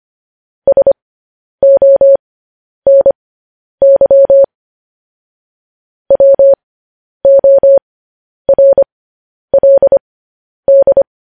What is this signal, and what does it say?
Morse code, heard as a keyed tone, about 25 wpm, spaced as for 10 wpm, 550 Hz. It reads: SONY WORLD
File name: podcast / News headlines-1x-25x10